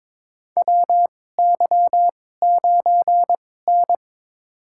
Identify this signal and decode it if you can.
Morse code, heard as a keyed tone, about 21 wpm, spaced as for 16 wpm, 700 Hz. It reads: WY9N
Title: USA Call Signs - 4 Characters - 2x Character Spacing - 22wpm